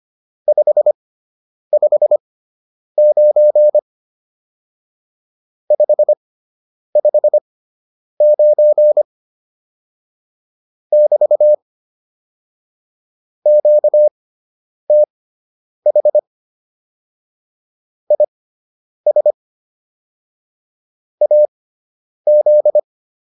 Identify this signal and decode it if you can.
Morse code, heard as a keyed tone, about 25 wpm, spaced as for 9 wpm, 600 Hz. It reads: 559 559 <BT> QTH IS AZ